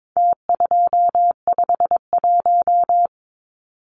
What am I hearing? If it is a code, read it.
Morse code, heard as a keyed tone, about 22 wpm, 700 Hz. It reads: T251
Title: Numbers - Contest Style - 4-Digit 22wpm